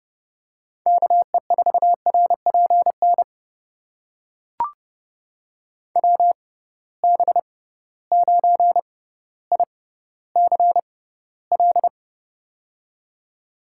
Morse code, heard as a keyed tone, about 30 wpm, spaced as for 15 wpm, 700 Hz. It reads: KE4RPN WB9ICL
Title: USA Call Signs - 6 Characters - 6x Character Spacing - 30wpm